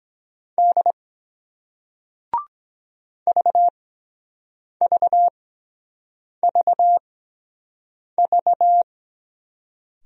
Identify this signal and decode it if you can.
Morse code, heard as a keyed tone, about 24 wpm, spaced as for 7 wpm, 700 Hz. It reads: D VVVV